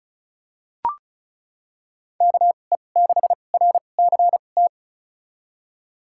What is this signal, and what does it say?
Morse code, heard as a keyed tone, about 36 wpm, spaced as for 25 wpm, 700 Hz. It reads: KE6RCT
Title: USA Call Signs - 2x Character Spacing - 35wpm